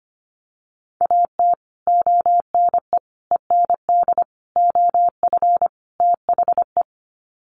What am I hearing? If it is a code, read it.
Morse code, heard as a keyed tone, about 25 wpm, 700 Hz. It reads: AT ONE END OF THE
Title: Sets of 5 Words Encore 25wpm